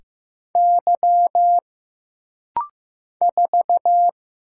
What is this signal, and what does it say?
Morse code, heard as a keyed tone, about 15 wpm, 700 Hz. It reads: Y 4
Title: Single Letter-Number Alt - Rapid-Fire 15wpm